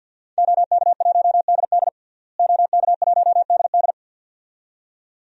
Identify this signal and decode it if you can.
Morse code, heard as a keyed tone, about 51 wpm, 700 Hz. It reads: OK1DD OK1DD
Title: DXCC-100 Call Signs - 5 Characters - Repeated 2x - 50wpm